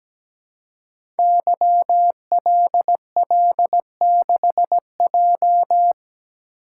Morse code, heard as a keyed tone, about 17 wpm, 700 Hz. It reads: YLL6J
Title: Code Groups - 5 Letter:Numbers - 17wpm